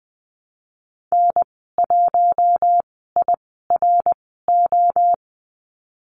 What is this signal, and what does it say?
Morse code, heard as a keyed tone, about 20 wpm, spaced as for 14 wpm, 700 Hz. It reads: N1IRO
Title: 062 Lesson 11g - Letter h - Call Signs 20wpm